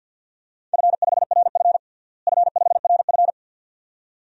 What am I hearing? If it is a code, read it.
Morse code, heard as a keyed tone, about 48 wpm, 700 Hz. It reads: F5RF F5RF
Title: DXCC-100 Call Signs - 4 Characters - Repeated 2x - 50wpm